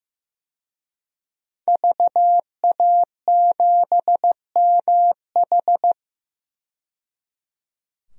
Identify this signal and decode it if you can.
Morse code, heard as a keyed tone, about 15 wpm, 700 Hz. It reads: VA7MH
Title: DXCC-100 Call Signs - 5 Characters - 15wpm